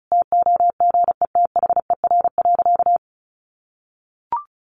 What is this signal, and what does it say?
Morse code, heard as a keyed tone, about 34 wpm, 700 Hz. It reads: TOGETHER.
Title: Sentences from Top 1400 Words 35wpm